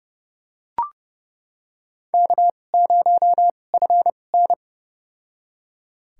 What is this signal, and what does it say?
Morse code, heard as a keyed tone, about 32 wpm, spaced as for 22 wpm, 700 Hz. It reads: K0FN
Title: USA Call Signs - 4 Characters - 2x Character Spacing - 30wpm